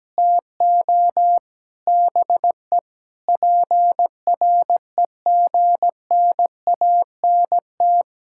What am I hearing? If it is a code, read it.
Morse code, heard as a keyed tone, about 17 wpm, 700 Hz. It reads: TO BE PREGNANT
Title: Sentences from Top 1000 Words - Review Part 6 - 17wpm